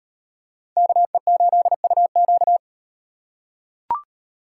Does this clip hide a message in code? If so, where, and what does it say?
Yes: Morse code from 0.8 to 2.6 s, heard as a keyed tone, about 38 wpm, 700 Hz. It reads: KE8UQ